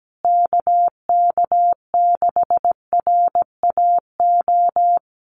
Morse code, heard as a keyed tone, about 17 wpm, 700 Hz. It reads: KK6RAO
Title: USA Call Signs Repeated 3x 17wpm